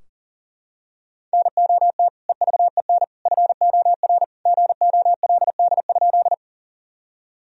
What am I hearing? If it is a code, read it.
Morse code, heard as a keyed tone, about 40 wpm, 700 Hz. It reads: NOT EVEN FOR GOLD?
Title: Sentences from Top 1000 Words - Review Part 3 - 40wpm